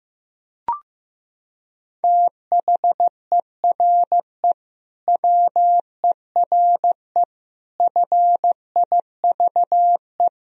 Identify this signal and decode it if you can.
Morse code, heard as a keyed tone, about 16 wpm, 700 Hz. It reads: THERE WERE FIVE
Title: Sentences from Top 1000 Words - Review Part 3 - 15wpm